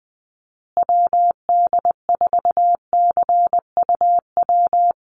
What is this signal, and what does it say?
Morse code, heard as a keyed tone, about 20 wpm, 700 Hz. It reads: WD4CUW
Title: USA Call Signs Repeated 3x 20wpm